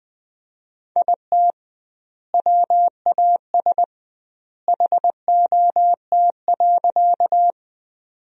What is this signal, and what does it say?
Morse code, heard as a keyed tone, about 20 wpm, 700 Hz. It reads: IT WAS HOT.
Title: 2x Word-Spacing - Sentences from Top 500 Words - 20wpm